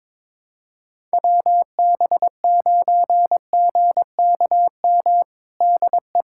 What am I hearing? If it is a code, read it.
Morse code, heard as a keyed tone, about 22 wpm, 700 Hz. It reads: WB9GKM DE
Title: USA Callsign DE Callsign - 6 Characters - 22wpm